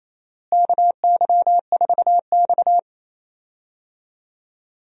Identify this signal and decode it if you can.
Morse code, heard as a keyed tone, about 28 wpm, 700 Hz. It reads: KY4X
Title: USA Call Signs - 4 Characters - Repeated 3x - 28wpm